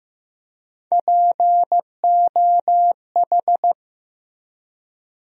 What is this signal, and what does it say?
Morse code, heard as a keyed tone, about 15 wpm, 700 Hz. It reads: POH
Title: USA Call Sign Postfixes - 3 Letters 15wpm